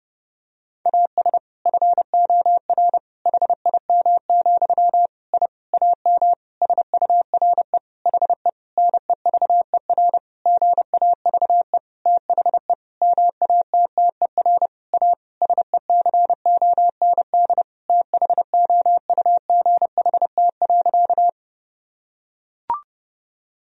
Morse code, heard as a keyed tone, about 32 wpm, 700 Hz. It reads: AS FOR HIM, I AM SURE HE NEVER GAVE THE MATTER A SECOND THOUGHT.
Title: Sentences from Top 1000 Words - Review Part 2 - 30wpm